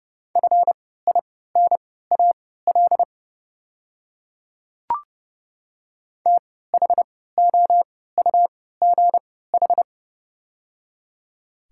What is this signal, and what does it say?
Morse code, heard as a keyed tone, about 30 wpm, spaced as for 17 wpm, 700 Hz. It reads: FINAL THOUGH
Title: 3x Character Spacing - Top 500 Words - 30wpm